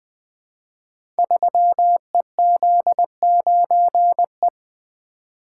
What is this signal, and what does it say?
Morse code, heard as a keyed tone, about 20 wpm, 700 Hz. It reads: 3EZ9E
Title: Code Groups - 5 Letter:Numbers - 20wpm